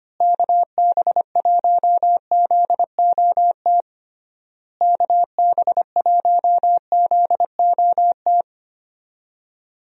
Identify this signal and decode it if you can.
Morse code, heard as a keyed tone, about 25 wpm, 700 Hz. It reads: KB1ZOT KB1ZOT
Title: USA Call Signs - 6 Characters - Repeated 3x - 25wpm